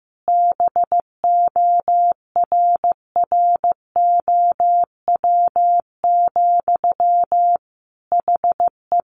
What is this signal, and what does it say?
Morse code, heard as a keyed tone, about 15 wpm, 700 Hz. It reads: BORROW, HE
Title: Sentences from Top 1500 Words 15wpm